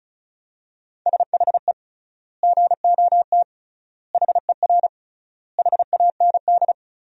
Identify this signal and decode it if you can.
Morse code, heard as a keyed tone, about 36 wpm, 700 Hz. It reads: SHE GOT HER HAND
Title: 3x Word-Spacing - Sentences from Top 500 Words - 35wpm